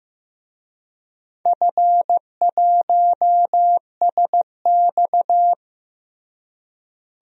Morse code, heard as a keyed tone, about 16 wpm, 700 Hz. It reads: F1SX
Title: DXCC-100 Call Signs - 4 Characters - Repeated 3x - 15wpm